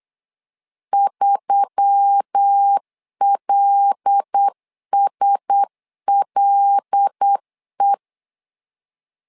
Morse code, heard as a keyed tone, about 8 wpm, 800 Hz. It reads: 3LSLE
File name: LEZIONI / 5 LBVC / 5 L 8wpm 2831